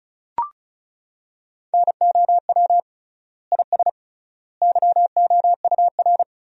Morse code, heard as a keyed tone, about 36 wpm, 700 Hz. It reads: NOW IS YOUR
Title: 3x Word-Spacing - Sentences from Top 100 Words - 35wpm